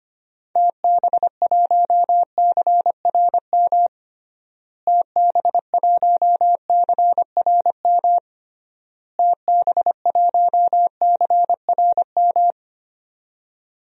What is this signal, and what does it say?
Morse code, heard as a keyed tone, about 25 wpm, 700 Hz. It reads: TB1CRM TB1CRM TB1CRM
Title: DXCC-100 Call Signs - 6 Characters - Repeated 3x - 25wpm